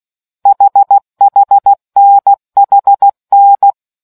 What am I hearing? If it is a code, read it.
Morse code, heard as a keyed tone, about 16 wpm, 800 Hz. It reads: HHNHN